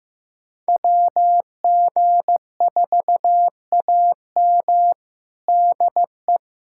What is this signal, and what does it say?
Morse code, heard as a keyed tone, about 15 wpm, 700 Hz. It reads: WG4AM DE
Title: USA Callsign DE Callsign 15wpm